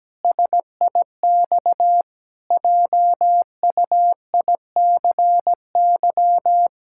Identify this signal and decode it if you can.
Morse code, heard as a keyed tone, about 17 wpm, 700 Hz. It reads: SIX JUICY
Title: Pangram Challenge v2 - 17wpm